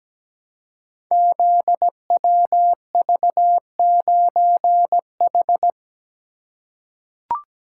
Morse code, heard as a keyed tone, about 17 wpm, 700 Hz. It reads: ZWV9H
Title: Code Groups - 5 Letter:Numbers - 17wpm